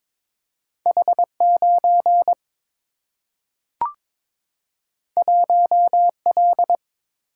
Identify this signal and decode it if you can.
Morse code, heard as a keyed tone, about 22 wpm, 700 Hz. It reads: H 9 1 L
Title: Single-Letter Single-Number 22wpm